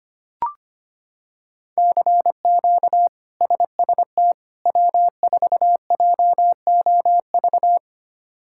Morse code, heard as a keyed tone, about 25 wpm, 700 Hz. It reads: CQ SST W4JOV
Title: K1USN SST QSO Practice - 25wpm